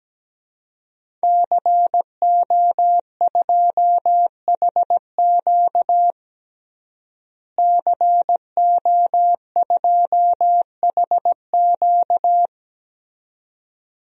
Morse code, heard as a keyed tone, about 17 wpm, 700 Hz. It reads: CO2HQ CO2HQ
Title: DXCC-100 Call Signs - 5 Characters - Repeated 2x - 17wpm